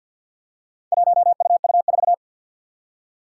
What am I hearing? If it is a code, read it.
Morse code, heard as a keyed tone, about 51 wpm, 700 Hz. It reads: 1UU4
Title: Four Challenging Letter-Number Combinations 50wpm